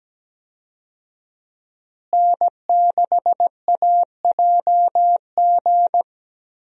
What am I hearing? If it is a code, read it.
Morse code, heard as a keyed tone, about 17 wpm, 700 Hz. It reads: N6AJG
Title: USA Call Signs - 5 Characters - Repeated 2x - 17wpm